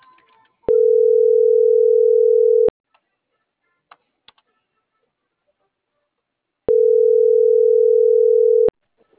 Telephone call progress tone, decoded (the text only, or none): ringback tone